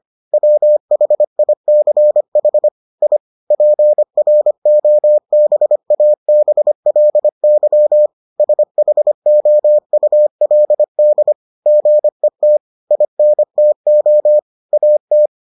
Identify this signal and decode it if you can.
Morse code, heard as a keyed tone, about 25 wpm, 600 Hz. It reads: WHICH I PROBABLY SHOULD GET INTO AT